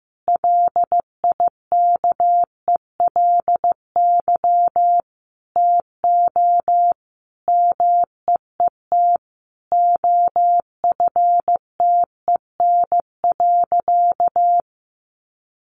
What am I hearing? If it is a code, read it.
Morse code, heard as a keyed tone, about 15 wpm, 700 Hz. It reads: LIKELY TO MEET OFTEN.
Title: Sentences from Top 1100 Words 15wpm